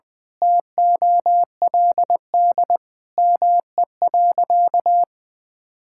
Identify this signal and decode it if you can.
Morse code, heard as a keyed tone, about 20 wpm, 700 Hz. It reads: TOLD ME.
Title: Sentences from Top 1000 Words - Review Part 6 - 20wpm